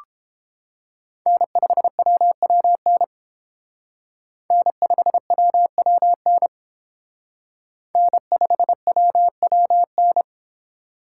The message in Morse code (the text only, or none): N5WWN N5WWN N5WWN